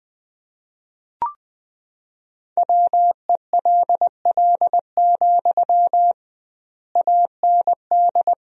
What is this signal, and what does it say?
Morse code, heard as a keyed tone, about 21 wpm, 700 Hz. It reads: WELL, AND
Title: 2x Word-Spacing - Sentences from Top 200 Words - 20wpm